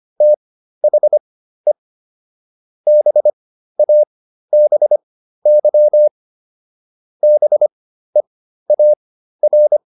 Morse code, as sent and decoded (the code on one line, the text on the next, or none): - .... / -... .- -... -.-- / -... .- .-.
THE BABY BEAR